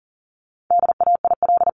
Morse code, heard as a keyed tone, about 40 wpm, 700 Hz. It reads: DAIL
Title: LY Suffix - Top 100 Words with Base Words 40wpm